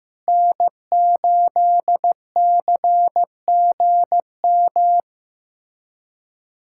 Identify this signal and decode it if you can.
Morse code, heard as a keyed tone, about 15 wpm, 700 Hz. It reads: N8CGM
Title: USA Callsign DE Callsign - 5 Characters - 15wpm